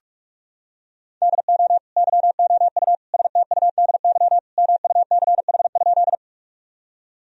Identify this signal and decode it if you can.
Morse code, heard as a keyed tone, about 41 wpm, 700 Hz. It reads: DO YOU STUDY MUCH?
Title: Sentences from Top 1000 Words - Review Part 1 - 45wpm